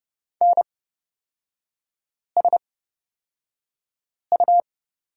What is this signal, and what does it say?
Morse code, heard as a keyed tone, about 30 wpm, spaced as for 5 wpm, 700 Hz. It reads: NSU
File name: Lesson 37e - Letter x - Three Characters 30wpm